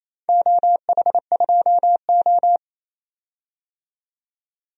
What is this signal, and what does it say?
Morse code, heard as a keyed tone, about 28 wpm, 690 Hz. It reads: OH2O
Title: DXCC-100 Call Signs - 4 Characters - 28wpm